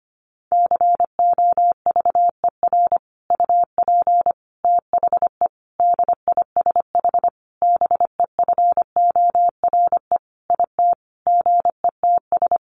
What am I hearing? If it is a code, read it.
Morse code, heard as a keyed tone, about 25 wpm, 700 Hz. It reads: COVER UP THE DISH BEFORE IT GETS
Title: Sentences from Top 1300 Words 25wpm